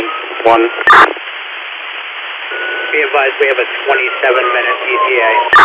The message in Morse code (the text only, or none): none